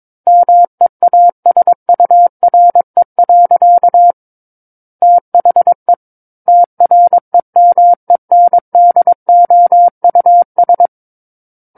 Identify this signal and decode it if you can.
Morse code, heard as a keyed tone, about 22 wpm, 700 Hz. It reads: MEASURE. THE TREMENDOUS
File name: Book One / 13-Chapter 13 - 22 WPM